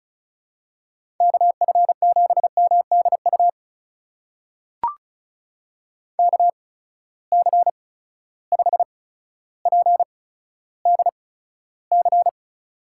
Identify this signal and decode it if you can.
Morse code, heard as a keyed tone, about 36 wpm, spaced as for 15 wpm, 700 Hz. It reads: KF7MDU KC5PDC
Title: USA Call Signs - 6 Characters - 8x Character Spacing - 35wpm